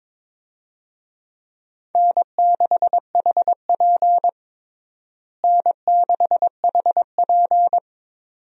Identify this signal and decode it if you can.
Morse code, heard as a keyed tone, about 22 wpm, 700 Hz. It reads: N6HP N6HP
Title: USA Call Signs - 4 Characters - Repeated 2x - 22wpm